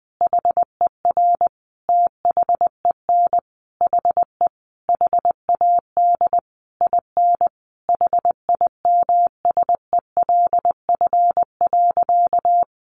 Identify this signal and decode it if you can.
Morse code, heard as a keyed tone, about 20 wpm, 700 Hz. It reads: HER THEN HE HAD IN HIMSELF.
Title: Sentences from Top 1500 Words 20wpm